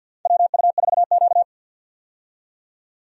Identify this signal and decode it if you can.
Morse code, heard as a keyed tone, about 52 wpm, 700 Hz. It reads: WU4Q
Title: USA Call Signs - 4 Characters - 50wpm